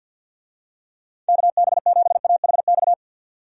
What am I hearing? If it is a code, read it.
Morse code, heard as a keyed tone, about 49 wpm, 700 Hz. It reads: KB7AHX
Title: USA Call Signs - 6 Characters - 50wpm